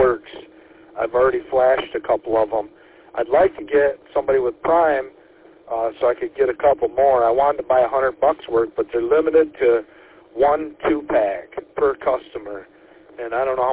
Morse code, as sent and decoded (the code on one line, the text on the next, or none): none